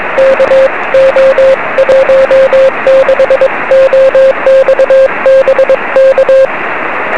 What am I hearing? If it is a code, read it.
Morse code, heard as a keyed tone, about 22 wpm, 530 Hz, with radio noise. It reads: KO16OX BK
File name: 1823z yl2aj 23cm